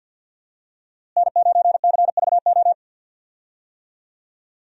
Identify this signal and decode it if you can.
Morse code, heard as a keyed tone, about 46 wpm, 700 Hz. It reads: N9CVO
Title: USA Call Signs - 5 Characters - Repeated 3x - 50wpm